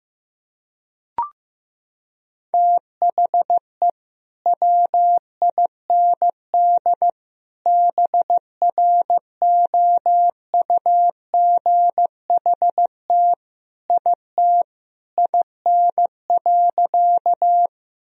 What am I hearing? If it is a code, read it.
Morse code, heard as a keyed tone, about 15 wpm, 700 Hz. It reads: THE WIND BROUGHT IT IN.
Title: Sentences from Top 1000 Words - Review Part 2 - 15wpm